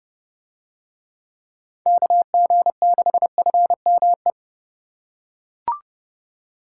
Morse code, heard as a keyed tone, about 30 wpm, 700 Hz. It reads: KG6FME